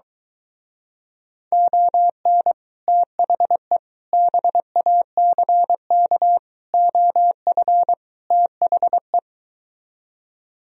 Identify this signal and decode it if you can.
Morse code, heard as a keyed tone, about 23 wpm, 700 Hz. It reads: ON THE BACK OF THE